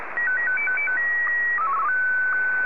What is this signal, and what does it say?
SSTV (slow-scan television), Martin 1.
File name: sstv